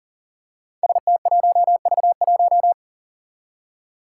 Morse code, heard as a keyed tone, about 40 wpm, 700 Hz. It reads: ST1V1
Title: Five Challenging Letter-Number Combinations 40wpm